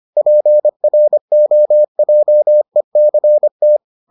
Morse code, heard as a keyed tone, about 25 wpm, 600 Hz. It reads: PROJECT